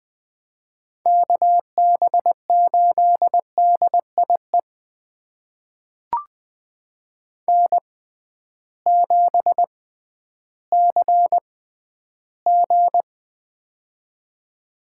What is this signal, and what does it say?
Morse code, heard as a keyed tone, about 20 wpm, spaced as for 12 wpm, 700 Hz. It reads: KB8DIE N7CG
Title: USA Call Signs - 6x Character Spacing - 20wpm